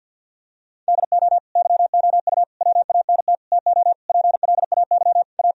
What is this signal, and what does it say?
Morse code, heard as a keyed tone, about 52 wpm, 700 Hz. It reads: DO YOU WANT TO PLAY A
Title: Sentences from Top 1000 Words - Review Part 3 - 50wpm